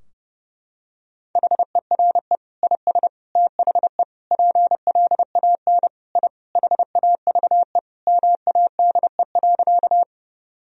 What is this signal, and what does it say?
Morse code, heard as a keyed tone, about 30 wpm, 700 Hz. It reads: HERE IS THE PLAN I HAVE MADE.
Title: Sentences from Top 1000 Words - Review Part 2 - 30wpm